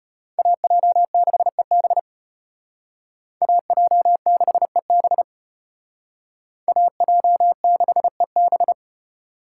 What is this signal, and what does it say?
Morse code, heard as a keyed tone, about 33 wpm, 700 Hz. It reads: AJ6EB AJ6EB AJ6EB